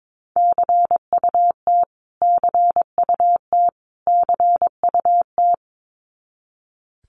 Morse code, heard as a keyed tone, about 22 wpm, 700 Hz. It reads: CUT CUT CUT
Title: Top 500 Words Repeated 22wpm